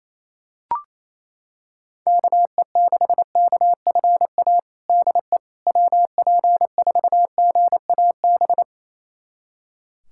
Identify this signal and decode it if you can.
Morse code, heard as a keyed tone, about 28 wpm, 700 Hz. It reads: KE6KFA DE WP4GAB
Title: USA Callsign DE Callsign - 6 Characters - 28wpm